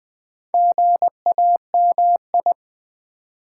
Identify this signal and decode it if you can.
Morse code, heard as a keyed tone, about 20 wpm, 700 Hz. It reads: GAMI
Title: Build A Word - 2 Letters At A Time - 6-Letter Words - 20wpm